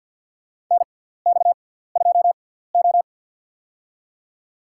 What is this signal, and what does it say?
Morse code, heard as a keyed tone, about 51 wpm, spaced as for 17 wpm, 700 Hz. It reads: NX2O
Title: USA Call Signs - 6x Character Spacing - 50wpm